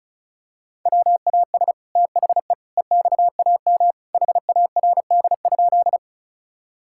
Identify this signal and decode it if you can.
Morse code, heard as a keyed tone, about 34 wpm, 700 Hz. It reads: WAS THE EXAM HARD?
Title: Sentences from Top 1000 Words - Review Part 5 - 35wpm